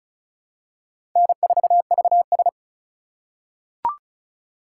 Morse code, heard as a keyed tone, about 36 wpm, 700 Hz. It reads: N4VS
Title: USA Call Signs - 4 Characters - 3x Character Spacing - 35wpm